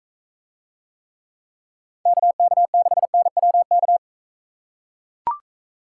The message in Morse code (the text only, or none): KK6NWK